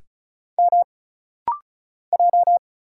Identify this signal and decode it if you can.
Morse code, heard as a keyed tone, about 34 wpm, 700 Hz. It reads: M J
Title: Single Letter-Number Alt - Rapid-Fire 35wpm